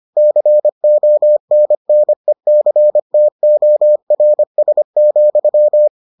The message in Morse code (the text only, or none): CONNECTORS,